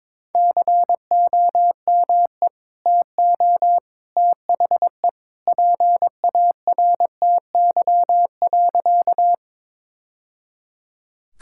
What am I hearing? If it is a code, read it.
Morse code, heard as a keyed tone, about 21 wpm, 700 Hz. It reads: COME TO THE PARTY.
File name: practice / 22wpm / Sentences with 2-Word Phrases - 22wpm